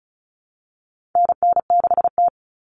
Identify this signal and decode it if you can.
Morse code, heard as a keyed tone, about 34 wpm, 700 Hz. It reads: NN6T
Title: Call Signs - 35wpm